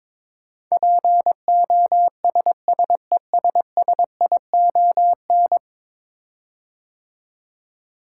Morse code, heard as a keyed tone, about 22 wpm, 700 Hz. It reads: POSSESSION
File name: practice / 22wpm / 10 Letter Words - First 3 Letters Given - 22wpm